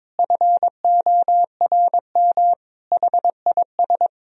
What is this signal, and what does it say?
Morse code, heard as a keyed tone, about 22 wpm, 700 Hz. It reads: FORM HIS
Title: Sentences from Top 1000 Words - Review Part 1 - 22wpm